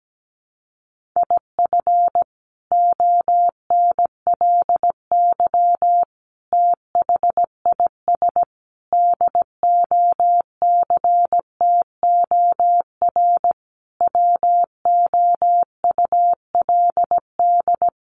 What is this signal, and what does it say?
Morse code, heard as a keyed tone, about 17 wpm, 700 Hz. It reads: IF ONLY THIS DOCTOR WOULD
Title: Sentences from Top 1500 Words 17wpm